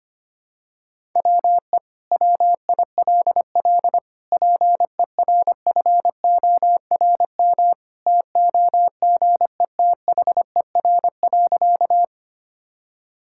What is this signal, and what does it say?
Morse code, heard as a keyed tone, about 25 wpm, 700 Hz. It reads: WE WILL PERFORM TOGETHER.
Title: Sentences from Top 1000 Words - Review Part 5 - 25wpm